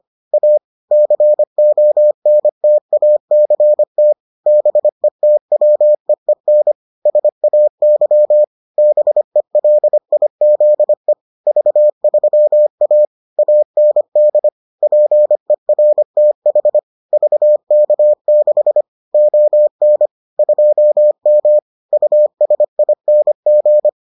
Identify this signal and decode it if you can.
Morse code, heard as a keyed tone, about 25 wpm, 600 Hz. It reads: A CONTACT BETWEEN SAY BELIZE V3A AND PERTH VK6 ON 2M USING